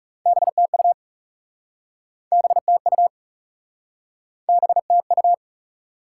Morse code, heard as a keyed tone, about 40 wpm, 700 Hz. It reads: BTU BTU BTU